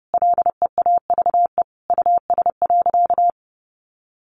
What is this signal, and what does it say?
Morse code, heard as a keyed tone, about 30 wpm, 700 Hz. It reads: LEAVE US.
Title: Sentences from Top 400 Words 30wpm